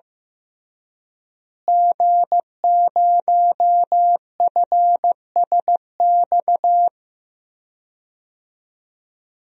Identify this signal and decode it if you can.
Morse code, heard as a keyed tone, about 15 wpm, 700 Hz. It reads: G0FSX
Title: DXCC-100 Call Signs - 5 Characters - Repeated 3x - 15wpm